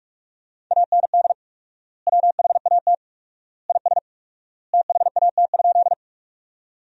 Morse code, heard as a keyed tone, about 41 wpm, 700 Hz. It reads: AND WHAT IS THAT?
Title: 4x Word-Spacing - Sentences from Top 100 Words - 45wpm